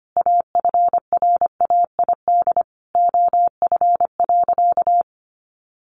Morse code, heard as a keyed tone, about 25 wpm, 700 Hz. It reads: AFRAID OF.
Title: Sentences from Top 1500 Words 25wpm